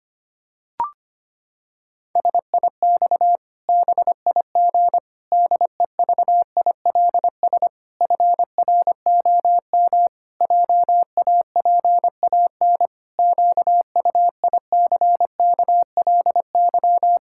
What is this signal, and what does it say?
Morse code, heard as a keyed tone, about 25 wpm, 700 Hz. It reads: SIX BIG DEVILS FROM JAPAN QUICKLY
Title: Pangram Challenge v2 - 25wpm